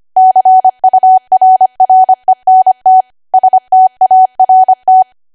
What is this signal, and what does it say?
Morse code, heard as a keyed tone, about 25 wpm, 750 Hz. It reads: CURRENT START